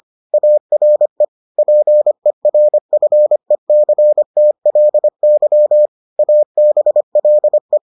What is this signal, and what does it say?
Morse code, heard as a keyed tone, about 25 wpm, 600 Hz. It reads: ARE PERFECTLY ABLE